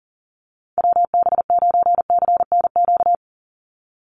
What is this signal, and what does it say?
Morse code, heard as a keyed tone, about 40 wpm, 700 Hz. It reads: WB9CNQ
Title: USA Call Signs 40wpm